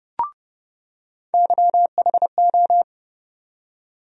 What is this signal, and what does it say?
Morse code, heard as a keyed tone, about 31 wpm, 700 Hz. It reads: YHO
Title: USA Call Sign Postfixes - 3 Letters 30wpm